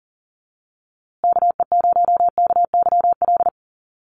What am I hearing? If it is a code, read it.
Morse code, heard as a keyed tone, about 40 wpm, 700 Hz. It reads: KE0KYL
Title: USA Call Signs Repeated 3x 40wpm